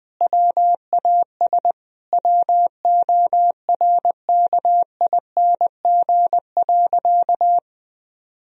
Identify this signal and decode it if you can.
Morse code, heard as a keyed tone, about 20 wpm, 700 Hz. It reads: WAS WORKING.
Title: Sentences from Top 1000 Words - Review Part 4 - 20wpm